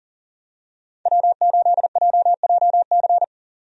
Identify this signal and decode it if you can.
Morse code, heard as a keyed tone, about 40 wpm, 700 Hz. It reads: W8JJC
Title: USA Call Signs - 5 Characters - 40wpm